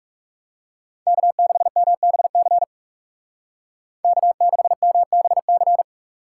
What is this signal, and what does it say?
Morse code, heard as a keyed tone, about 39 wpm, 700 Hz. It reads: K6MBC K6MBC